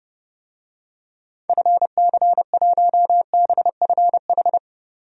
Morse code, heard as a keyed tone, about 32 wpm, 700 Hz. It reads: FC1BFH
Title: DXCC-100 Call Signs - 6 Characters - 30wpm